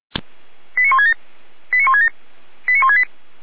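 Five-tone selective call (ZVEI), heard as quick runs of five tones, tones of about 80 ms. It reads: 89167 89167 89168
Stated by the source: ZVEI five-tone selective call 89084 received by SDR.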